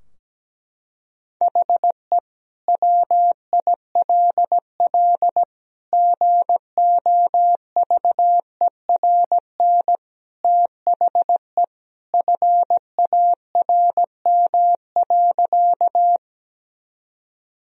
Morse code, heard as a keyed tone, about 17 wpm, 700 Hz. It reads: HE WILL GOVERN THE FARM.
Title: Sentences from Top 1000 Words - Review Part 2 - 17wpm